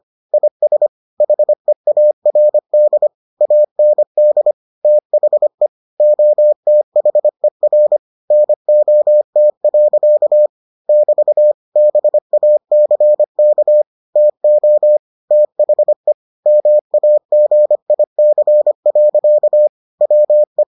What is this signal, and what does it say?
Morse code, heard as a keyed tone, about 25 wpm, 600 Hz. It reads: IS HEARD AND THE OTHER NOT. <BT> BACK TO THE MAGIC. WE